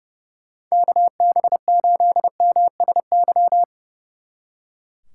Morse code, heard as a keyed tone, about 36 wpm, 700 Hz. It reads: KB8MSY
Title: Speed Racing - USA Call Signs 30wpm